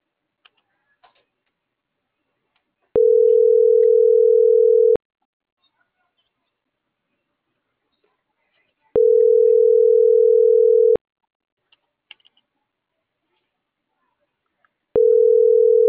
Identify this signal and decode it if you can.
Ringback tone (North America), 440 + 480 Hz.